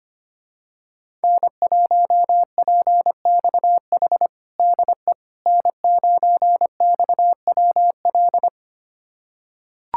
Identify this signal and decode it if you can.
Morse code, heard as a keyed tone, about 25 wpm, 700 Hz. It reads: N1PXH DE N9XWL
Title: USA Callsign DE Callsign - 5 Characters - 25wpm